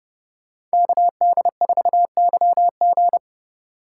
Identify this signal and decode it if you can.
Morse code, heard as a keyed tone, about 37 wpm, 700 Hz. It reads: KD4YG
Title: Speed Racing - USA 2x2 Call Signs 30wpm